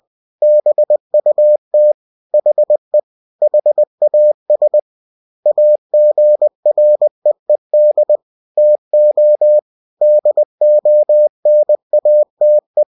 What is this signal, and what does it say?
Morse code, heard as a keyed tone, about 20 wpm, 600 Hz. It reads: BUT HE HAS AGREED TO DONATE
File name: cw-books / winnie-the-pooh / winnie-20wpm-page-0117